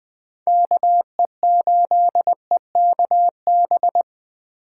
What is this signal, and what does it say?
Morse code, heard as a keyed tone, about 20 wpm, 700 Hz. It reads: KE8EKB
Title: Speed Racing - USA 2x3 Call Signs 17wpm